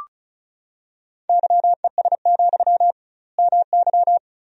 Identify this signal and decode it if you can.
Morse code, heard as a keyed tone, about 34 wpm, 700 Hz. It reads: YES, MY
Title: 2x Word-Spacing - Sentences from Top 500 Words - 35wpm